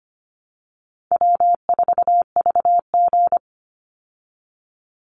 Morse code, heard as a keyed tone, about 25 wpm, 700 Hz. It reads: W4VG
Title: Easy Call Signs Repeated 2x - 25wpm